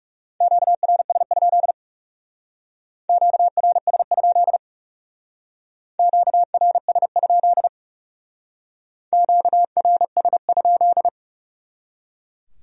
Morse code, heard as a keyed tone, about 37 wpm, 700 Hz. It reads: QRS? QRS? QRS? QRS?
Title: Speed-Racing Q-Codes - 30wpm